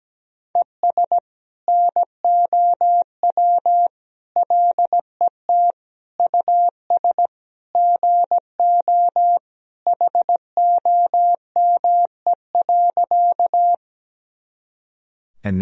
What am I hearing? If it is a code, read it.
Morse code, heard as a keyed tone, about 17 wpm, 700 Hz. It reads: ES NOW LET US GO HOME.
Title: Sentences from Top 300 Words and Abbreviations 17wpm